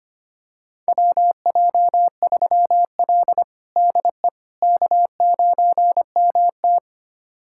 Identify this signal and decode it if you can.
Morse code, heard as a keyed tone, about 25 wpm, 700 Hz. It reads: WJ3L DE K9MT
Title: USA Callsign DE Callsign - 4 Characters - 25wpm